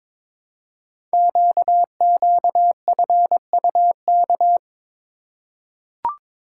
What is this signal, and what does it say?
Morse code, heard as a keyed tone, about 22 wpm, 700 Hz. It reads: QQFUK
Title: Code Groups - 5 Letters - 22wpm